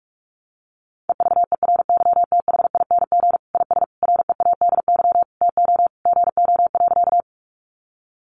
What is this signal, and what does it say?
Morse code, heard as a keyed tone, about 44 wpm, 700 Hz. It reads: EVERYTHING IS READY TO GO.
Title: Sentences from Top 700 Words 45wpm